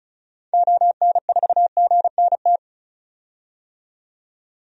Morse code, heard as a keyed tone, about 34 wpm, 700 Hz. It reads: ON4GNT
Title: DXCC-100 Call Signs - 6 Characters - 35wpm